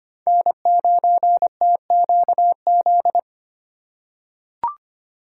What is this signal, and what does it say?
Morse code, heard as a keyed tone, about 25 wpm, 700 Hz. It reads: N9TQZ
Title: USA Call Signs - 5 Characters - 2x Character Spacing - 25wpm